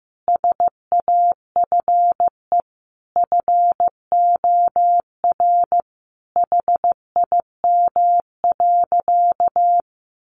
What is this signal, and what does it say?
Morse code, heard as a keyed tone, about 15 wpm, 700 Hz. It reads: SAFE FOR HIM.
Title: Sentences from Top 900 Words 15wpm